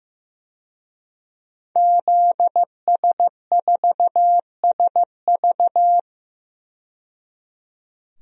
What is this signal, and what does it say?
Morse code, heard as a keyed tone, about 15 wpm, 700 Hz. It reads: ZS4SV
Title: DXCC-100 Call Signs - 5 Characters - 15wpm